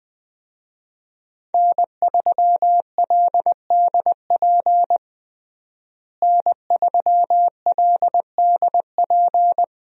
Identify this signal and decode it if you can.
Morse code, heard as a keyed tone, about 20 wpm, 700 Hz. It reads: N3LDP N3LDP
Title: USA Call Signs - 5 Characters - Repeated 3x - 20wpm